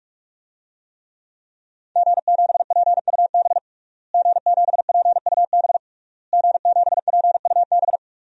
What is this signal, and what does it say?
Morse code, heard as a keyed tone, about 41 wpm, 700 Hz. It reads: G7PUB G7PUB G7PUB
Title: DXCC-100 Call Signs - 5 Characters - Repeated 3x - 45wpm